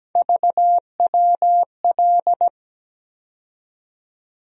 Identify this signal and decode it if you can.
Morse code, heard as a keyed tone, about 17 wpm, 700 Hz. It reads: VWL
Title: USA Call Sign Postfixes - 3 Letters 17wpm